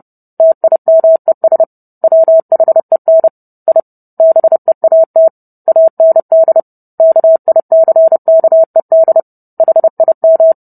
Morse code, heard as a keyed tone, about 30 wpm, 650 Hz. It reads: TIMES WHEN I BEAT AND KICKED HIM